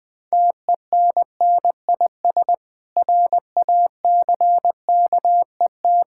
Morse code, heard as a keyed tone, about 20 wpm, 700 Hz. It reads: TENNIS RACKET
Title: Difficult Words with Hints - 20wpm